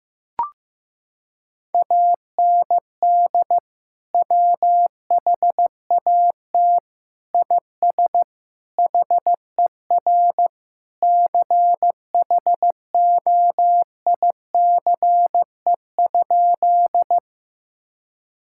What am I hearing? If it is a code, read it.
Morse code, heard as a keyed tone, about 15 wpm, 700 Hz. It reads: AND WHAT IS HER CHOICE?
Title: Sentences from Top 1000 Words - Review Part 6 - 15wpm